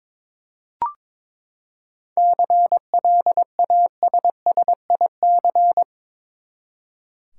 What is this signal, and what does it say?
Morse code, heard as a keyed tone, about 22 wpm, 700 Hz. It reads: CLASSIC
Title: Seven Letter Words - 22wpm